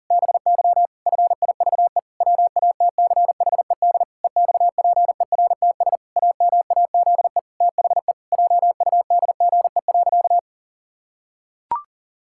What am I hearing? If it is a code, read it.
Morse code, heard as a keyed tone, about 40 wpm, 700 Hz. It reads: BY FIVE WATCHED EXPERTS AMAZE THE JUDGE.
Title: Pangram Challenge v2 - 40wpm